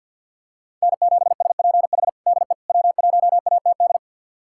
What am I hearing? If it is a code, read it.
Morse code, heard as a keyed tone, about 48 wpm, 700 Hz. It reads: N7SPH DE W1ATD
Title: USA Callsign DE Callsign - 5 Characters - 50wpm